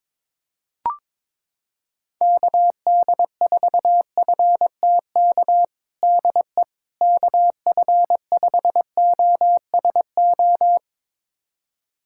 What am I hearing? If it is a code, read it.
Morse code, heard as a keyed tone, about 22 wpm, 700 Hz. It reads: KD4FTK DE KF5OSO
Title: USA Callsign DE Callsign 22wpm